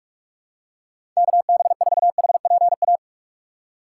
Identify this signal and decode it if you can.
Morse code, heard as a keyed tone, about 46 wpm, 700 Hz. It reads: KB4HPA
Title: USA Call Signs - 6 Characters - 4x Character Spacing - 45wpm